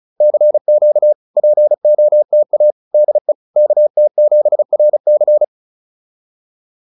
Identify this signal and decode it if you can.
Morse code, heard as a keyed tone, about 37 wpm, 600 Hz. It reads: CQ POTA DE KT7RC